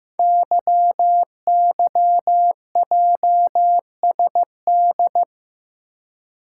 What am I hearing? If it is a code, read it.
Morse code, heard as a keyed tone, about 15 wpm, 700 Hz. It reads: YYJSD
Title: Code Groups - 5 Letters - 15wpm